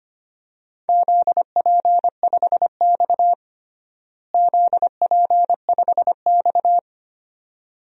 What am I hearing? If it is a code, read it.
Morse code, heard as a keyed tone, about 25 wpm, 700 Hz. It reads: ZP5X ZP5X
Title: DXCC-100 Call Signs - 4 Characters - Repeated 2x - 25wpm